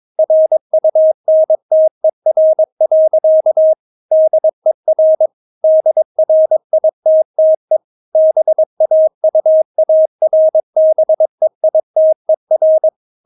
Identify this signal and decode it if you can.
Morse code, heard as a keyed tone, about 22 wpm, 630 Hz. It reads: RUNTER. DER DRITTE BAUARBEITER
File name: Witz3 22wpm